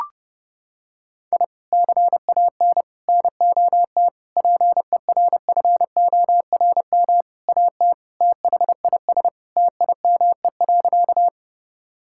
Morse code, heard as a keyed tone, about 32 wpm, 700 Hz. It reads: I CAN NOT PERFORM AT THIS TIME.
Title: Sentences from Top 1000 Words - Review Part 5 - 30wpm